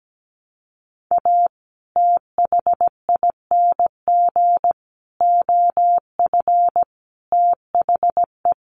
Morse code, heard as a keyed tone, about 17 wpm, 700 Hz. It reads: A THING OF THE
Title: Sets of 5 Words Encore 17wpm